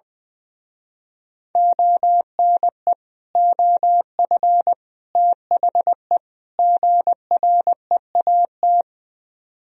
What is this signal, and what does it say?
Morse code, heard as a keyed tone, about 21 wpm, 700 Hz. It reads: ONE OF THE GREAT
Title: Speed-Racing - Sets of 4 Words 20wpm